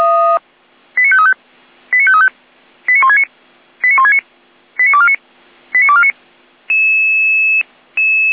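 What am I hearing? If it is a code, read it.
Five-tone selective call (ZVEI), heard as quick runs of five tones, tones of about 80 ms. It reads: 89535 89535 89169 89169 89249 89249
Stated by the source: Five-tone ZVEI selective call received on an SDR.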